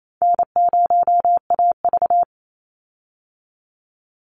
Morse code, heard as a keyed tone, about 28 wpm, 700 Hz. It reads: N0AV
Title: Easy Call Signs Repeated 3x - 28wpm